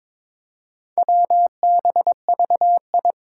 Morse code, heard as a keyed tone, about 22 wpm, 700 Hz. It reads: WBVI
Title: Four Challenging Letter-Number Combinations 22wpm